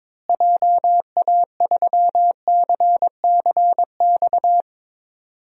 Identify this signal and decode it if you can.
Morse code, heard as a keyed tone, about 22 wpm, 700 Hz. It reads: JA3CCX
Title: DXCC-100 Call Signs - 6 Characters - Repeated 3x - 22wpm